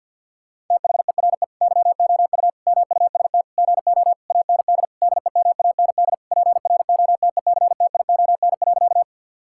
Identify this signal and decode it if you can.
Morse code, heard as a keyed tone, about 50 wpm, 700 Hz. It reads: THERE YOU MUST GO AND DEMAND PROTECTION.
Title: Sentences from Top 1600 Words 50wpm